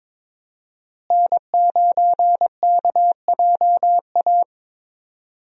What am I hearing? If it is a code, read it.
Morse code, heard as a keyed tone, about 21 wpm, 700 Hz. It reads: N9KJA
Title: USA Call Signs - 5 Characters - Repeated 2x - 22wpm